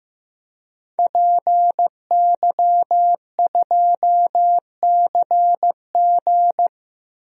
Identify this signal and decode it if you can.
Morse code, heard as a keyed tone, about 15 wpm, 700 Hz. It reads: PY2CG
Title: DXCC-100 Call Signs - 5 Characters - 15wpm